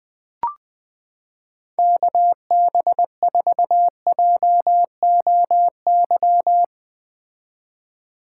Morse code, heard as a keyed tone, about 21 wpm, 700 Hz. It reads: KB4JOY
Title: 2023 POTA Activator Call Signs - 20wpm